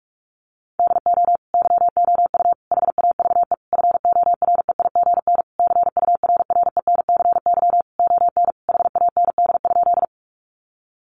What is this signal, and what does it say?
Morse code, heard as a keyed tone, about 44 wpm, 700 Hz. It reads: DO YOU HAVE FOREIGN CURRENCY ON HAND?
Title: Sentences from Top 1500 Words 45wpm